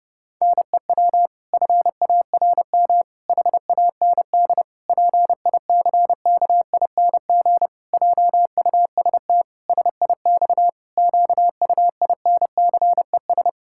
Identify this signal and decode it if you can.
Morse code, heard as a keyed tone, about 30 wpm, 700 Hz. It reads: NEW FARM HAND PICKING JUST SIX QUINCES
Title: Pangram Challenge v2 - 30wpm